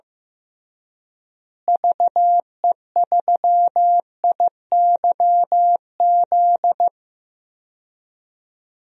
Morse code, heard as a keyed tone, about 15 wpm, 700 Hz. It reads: VE3IYZ
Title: DXCC-100 Call Signs - 6 Characters - Repeated 3x - 15wpm